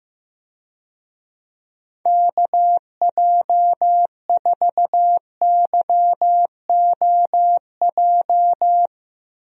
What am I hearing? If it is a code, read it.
Morse code, heard as a keyed tone, about 15 wpm, 700 Hz. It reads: KJ4YOJ
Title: USA Call Signs - 6 Characters - 15wpm